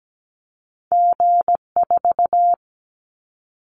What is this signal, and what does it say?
Morse code, heard as a keyed tone, about 17 wpm, 700 Hz. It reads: G4
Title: Callsign Prefixes for Top 25 DXCC - Repeated 3x - 17wpm